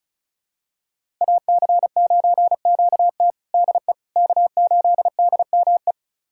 Morse code, heard as a keyed tone, about 36 wpm, 700 Hz. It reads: AC9QT DE K8DME
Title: USA Callsign DE Callsign - 5 Characters - 35wpm